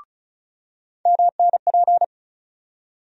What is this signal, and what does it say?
Morse code, heard as a keyed tone, about 34 wpm, 700 Hz. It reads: MNP
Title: USA Call Sign Postfixes - 3 Letters 35wpm